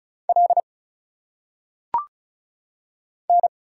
Morse code, heard as a keyed tone, about 36 wpm, 700 Hz. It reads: L N
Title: Single Letters - Phonetic Pronunciation 35wpm